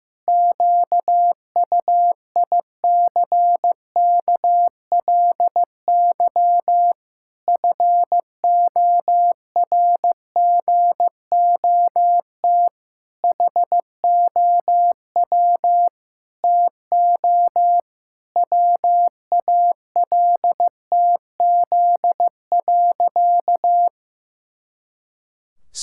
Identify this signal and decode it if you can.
Morse code, heard as a keyed tone, about 15 wpm, 700 Hz. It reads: QUICKLY FORGOT HOW TO WALTZ.
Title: Pangram Challenge v2 - 15wpm